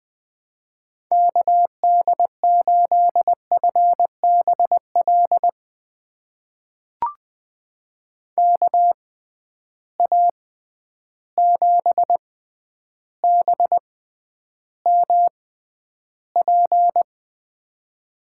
Morse code, heard as a keyed tone, about 21 wpm, spaced as for 10 wpm, 700 Hz. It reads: KD8FBL KA7BMP
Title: USA Call Signs - 6 Characters - 6x Character Spacing - 20wpm